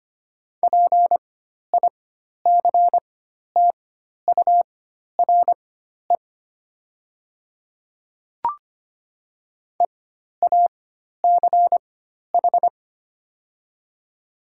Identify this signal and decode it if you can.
Morse code, heard as a keyed tone, about 25 wpm, spaced as for 12 wpm, 700 Hz. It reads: PICTURE EACH
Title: 4x Character Spacing - Top 300 Words - 25wpm